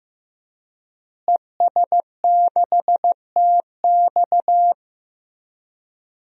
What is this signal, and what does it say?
Morse code, heard as a keyed tone, about 15 wpm, 700 Hz. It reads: ES6TX
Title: DXCC-100 Call Signs - 5 Characters - Repeated 2x - 15wpm